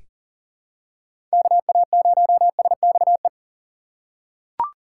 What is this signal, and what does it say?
Morse code, heard as a keyed tone, about 40 wpm, 700 Hz. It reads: KA0SXE
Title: USA Call Signs - 6 Characters - 3x Character Spacing - 40wpm